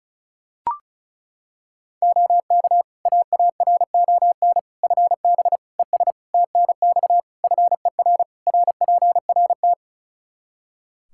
Morse code, heard as a keyed tone, about 36 wpm, 700 Hz. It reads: OK AARON FB ES TNX FER RPRT
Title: Practice QSO - Exchange 2 - Info - 35wpm